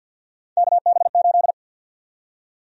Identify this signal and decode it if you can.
Morse code, heard as a keyed tone, about 52 wpm, 700 Hz. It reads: KB8
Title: USA Call Sign Prefixes - 50wpm